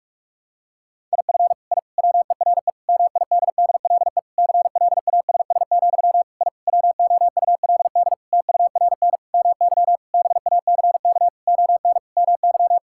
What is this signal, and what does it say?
Morse code, heard as a keyed tone, about 46 wpm, 700 Hz. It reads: IF I WERE MIDDLE CLASS, I WOULD TURN MY BACK ON MY
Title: Sentences from Top 1000 Words - Review Part 6 - 45wpm